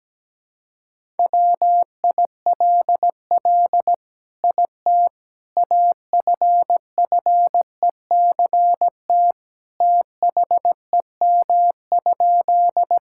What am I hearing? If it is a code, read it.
Morse code, heard as a keyed tone, about 17 wpm, 700 Hz. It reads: WILL IT AFFECT THEM?
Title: Sentences from Top 1000 Words - Review Part 6 - 17wpm